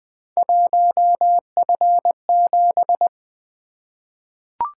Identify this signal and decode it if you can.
Morse code, heard as a keyed tone, about 20 wpm, 700 Hz. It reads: 1F7
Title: Three Letter-Number Combinations - 20wpm